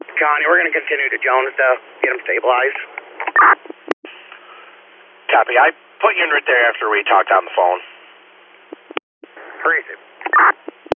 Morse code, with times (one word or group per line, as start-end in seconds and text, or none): none